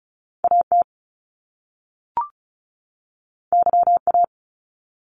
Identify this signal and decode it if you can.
Morse code, heard as a keyed tone, about 36 wpm, 700 Hz. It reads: AT YA
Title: Top Two Letter Combinations in Words 35wpm